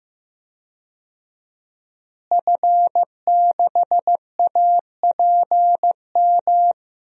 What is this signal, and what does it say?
Morse code, heard as a keyed tone, about 15 wpm, 700 Hz. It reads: F6APM
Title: DXCC-100 Call Signs - 5 Characters - Repeated 3x - 15wpm